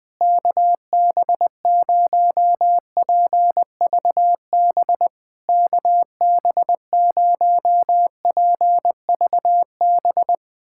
Morse code, heard as a keyed tone, about 20 wpm, 700 Hz. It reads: KB0PVB KB0PVB
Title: POTA QSO Practice - 20wpm